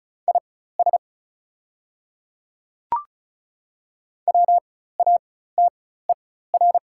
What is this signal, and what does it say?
Morse code, heard as a keyed tone, about 33 wpm, spaced as for 16 wpm, 700 Hz. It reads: IS WATER
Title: 4x Character Spacing - Top 300 Words - 35wpm